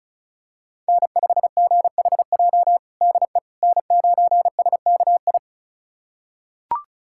Morse code, heard as a keyed tone, about 36 wpm, 700 Hz. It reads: N5GHJ DE N9SKI
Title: USA Callsign DE Callsign - 5 Characters - 35wpm